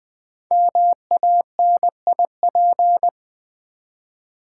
Morse code, heard as a keyed tone, about 20 wpm, 700 Hz. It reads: MANIP
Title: Build A Word - 5 Letters At A Time - 12-Letter Words - 20wpm